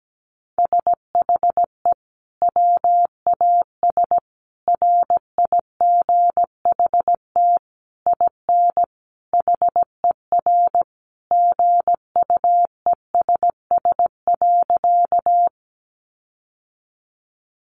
Morse code, heard as a keyed tone, about 17 wpm, 700 Hz. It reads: SHE WAS RIGHT IN HER GUESS.
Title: Sentences from Top 1500 Words 17wpm